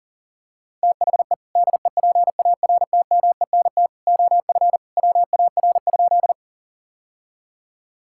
Morse code, heard as a keyed tone, about 40 wpm, 700 Hz. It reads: THE DEPARTMENT OF WAR?
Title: Sentences from Top 1000 Words - Review Part 5 - 40wpm